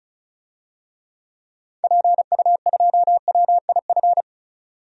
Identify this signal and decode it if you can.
Morse code, heard as a keyed tone, about 34 wpm, 700 Hz. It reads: PU2WIF
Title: DXCC-100 Call Signs - 6 Characters - 35wpm